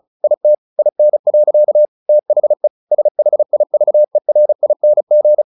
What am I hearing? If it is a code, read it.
Morse code, heard as a keyed tone, about 34 wpm, 600 Hz. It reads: IT IN. THE SHIVERING
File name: the old man and the sea / the old man and the sea 1 35wpm